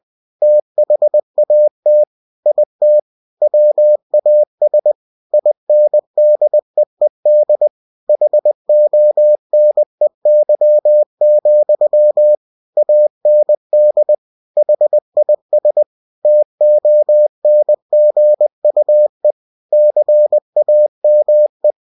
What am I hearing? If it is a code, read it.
Morse code, heard as a keyed tone, about 20 wpm, 600 Hz. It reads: THAT IT WAS INDEED HONEY, AND HIS TONGUE CAME
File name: cw-books / winnie-the-pooh / winnie-20wpm-page-0040